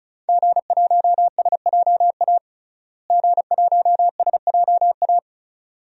Morse code, heard as a keyed tone, about 34 wpm, 700 Hz. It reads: G1SJA G1SJA
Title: DXCC-100 Call Signs - 5 Characters - Repeated 2x - 35wpm